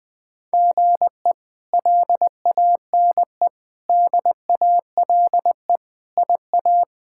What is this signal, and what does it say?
Morse code, heard as a keyed tone, about 20 wpm, 700 Hz. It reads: GE LANE DALE IA
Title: K1USN SST QSO Practice - 20wpm